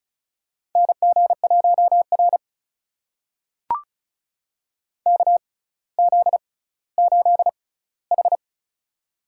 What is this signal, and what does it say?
Morse code, heard as a keyed tone, about 36 wpm, spaced as for 18 wpm, 700 Hz. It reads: NG1R KZ8H
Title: USA Call Signs - 4 Characters - 6x Character Spacing - 35wpm